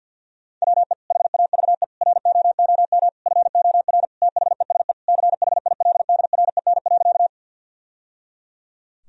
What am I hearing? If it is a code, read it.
Morse code, heard as a keyed tone, about 49 wpm, 700 Hz. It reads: WE HAVE ROOM FOR THESE CHILDREN.
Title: Sentences from Top 1000 Words - Review Part 1 - 50wpm